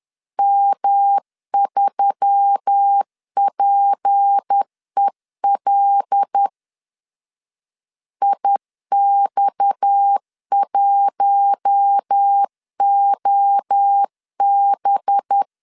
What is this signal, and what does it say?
Morse code, heard as a keyed tone, about 10 wpm, 800 Hz. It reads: M3PEL IX1OB
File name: CW / DET / 1 6 1500C S B 10wpm 4137